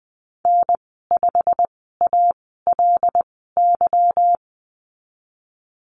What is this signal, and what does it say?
Morse code, heard as a keyed tone, about 20 wpm, spaced as for 15 wpm, 700 Hz. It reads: N5ALY
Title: 186 Lesson 30g - Letter b - Call Signs 20wpm